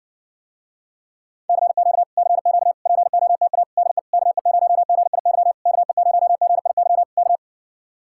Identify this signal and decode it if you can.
Morse code, heard as a keyed tone, about 70 wpm, 700 Hz. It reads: CQ CQ POTA DE KE0GIY KE0GIY K